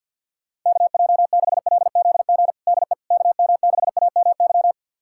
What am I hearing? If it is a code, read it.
Morse code, heard as a keyed tone, about 52 wpm, 700 Hz. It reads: KJ6LZG DE KM6AMY
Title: USA Callsign DE Callsign - 6 Characters - 50wpm